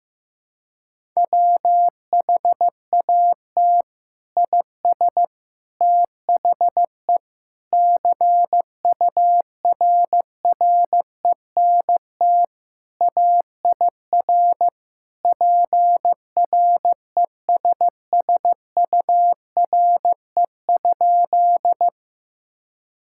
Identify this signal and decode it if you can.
Morse code, heard as a keyed tone, about 15 wpm, 700 Hz. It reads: WHAT IS THE CURRENT AIR PRESSURE?
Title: Sentences from Top 1000 Words - Review Part 6 - 15wpm